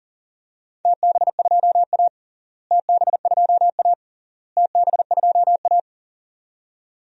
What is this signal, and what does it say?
Morse code, heard as a keyed tone, about 40 wpm, 700 Hz. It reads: TB2A TB2A TB2A
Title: DXCC-100 Call Signs - 4 Characters - Repeated 3x - 40wpm